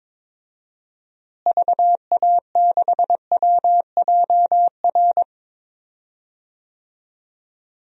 Morse code, heard as a keyed tone, about 22 wpm, 700 Hz. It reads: VA6WJR
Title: DXCC-100 Call Signs - 6 Characters - Repeated 2x - 22wpm